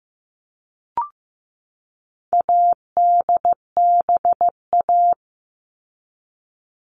Morse code, heard as a keyed tone, about 15 wpm, 700 Hz. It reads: ADBA
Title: All Cut Numbers - 4-Digit 15wpm